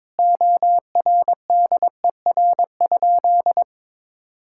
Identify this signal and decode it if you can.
Morse code, heard as a keyed tone, about 22 wpm, 700 Hz. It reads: ORDER?
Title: Sentences from Top 1000 Words - Review Part 6 - 22wpm